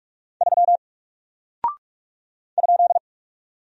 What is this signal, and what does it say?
Morse code, heard as a keyed tone, about 44 wpm, 700 Hz. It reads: 3 ?
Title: Single Character (Letter:Number:Punctuation:Prosign) Alt 45wpm